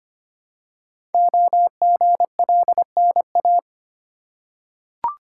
Morse code, heard as a keyed tone, about 25 wpm, 700 Hz. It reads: OGLNA
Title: Code Groups - 5 Letters - 25wpm